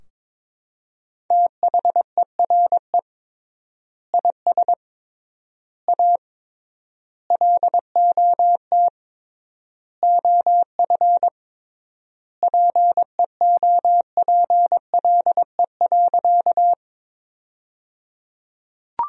Morse code, heard as a keyed tone, about 22 wpm, 700 Hz. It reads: THERE IS A LOT OF PEOPLE.
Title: 3x Word-Spacing - Sentences from Top 500 Words - 22wpm